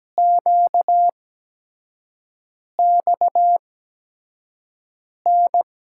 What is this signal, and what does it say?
Morse code, heard as a keyed tone, about 17 wpm, spaced as for 5 wpm, 700 Hz. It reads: QXN
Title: USA Call Signs - 5 Characters - 8x Character Spacing - 17wpm